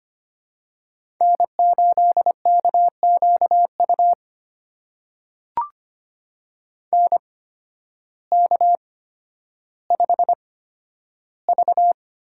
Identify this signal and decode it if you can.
Morse code, heard as a keyed tone, about 25 wpm, spaced as for 12 wpm, 700 Hz. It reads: N8KQU NK5V
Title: USA Call Signs - 8x Character Spacing - 25wpm